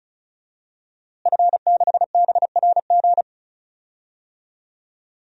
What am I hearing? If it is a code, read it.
Morse code, heard as a keyed tone, about 36 wpm, 700 Hz. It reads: F6BRG
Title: DXCC-100 Call Signs - 5 Characters - Repeated 2x - 35wpm